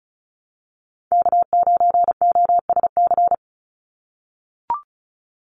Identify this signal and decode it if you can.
Morse code, heard as a keyed tone, about 34 wpm, 700 Hz. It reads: K9OSC
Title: NA SOTA Chaser Callsigns 35wpm